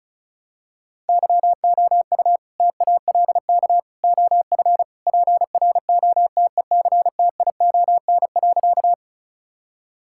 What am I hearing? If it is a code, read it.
Morse code, heard as a keyed tone, about 34 wpm, 700 Hz. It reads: YOU TALK OF PROTECTION.
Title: Sentences from Top 1600 Words 35wpm